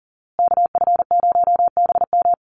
Morse code, heard as a keyed tone, about 40 wpm, 700 Hz. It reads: KF0BM
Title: USA Call Signs Repeated 2x 40wpm